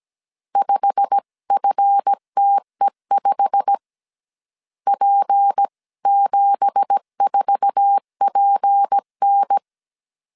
Morse code, heard as a keyed tone, about 17 wpm, 800 Hz. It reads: 5FTE5 P74PN
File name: CW / DET / 1 9 1500C S C 16wpm 2909